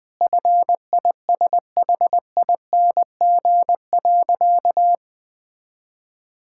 Morse code, heard as a keyed tone, about 20 wpm, 700 Hz. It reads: FISHING.
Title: Sentences from Top 1000 Words - Review Part 6 - 20wpm